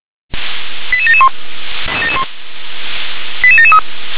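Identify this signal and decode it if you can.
Five-tone selective call (ZVEI), heard as quick runs of five tones, tones of about 70 ms. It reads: 88771 88773
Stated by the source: ZVEI five-tone call received on an SDR.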